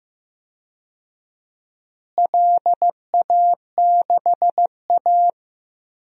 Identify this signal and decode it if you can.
Morse code, heard as a keyed tone, about 15 wpm, 700 Hz. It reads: LA6A
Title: DXCC-100 Call Signs - 4 Characters - Repeated 3x - 15wpm